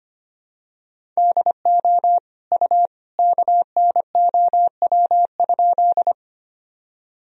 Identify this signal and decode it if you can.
Morse code, heard as a keyed tone, about 25 wpm, 700 Hz. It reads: DO U KNOW?
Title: Sentences from Top 100 Words and Abbreviations 25wpm